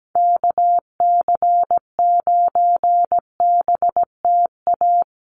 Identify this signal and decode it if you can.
Morse code, heard as a keyed tone, about 17 wpm, 700 Hz. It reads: KC9BTA
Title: USA Call Signs Repeated 2x 17wpm